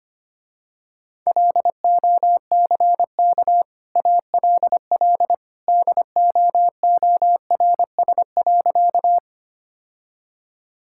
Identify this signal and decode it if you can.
Morse code, heard as a keyed tone, about 25 wpm, 700 Hz. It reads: LOCK ALL DOORS.
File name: Easy Sentences with Hints - 25wpm